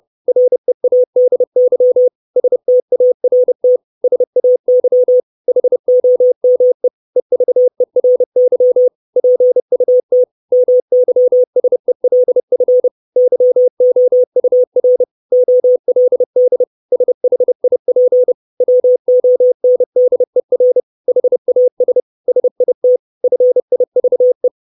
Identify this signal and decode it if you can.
Morse code, heard as a keyed tone, about 30 wpm, 500 Hz. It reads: READY START SAY HOME EVERY PUT MYSELF YOUR OLD SHIP WONDER HAS SIT FIVE
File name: 632 Words 30WPM